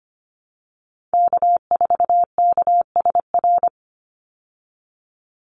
Morse code, heard as a keyed tone, about 25 wpm, 700 Hz. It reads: K4KSR
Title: Call Signs Repeated 3x - 25wpm